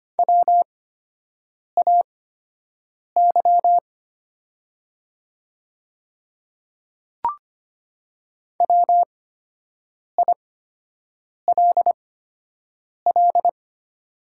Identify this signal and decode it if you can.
Morse code, heard as a keyed tone, about 25 wpm, spaced as for 7 wpm, 700 Hz. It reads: WAY WILL
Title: 8x Character Spacing - Top 100 Words - 25wpm